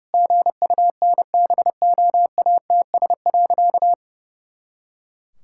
Morse code, heard as a keyed tone, about 32 wpm, 700 Hz. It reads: GUNBOATS.
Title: Pangram Challenge v2 - 30wpm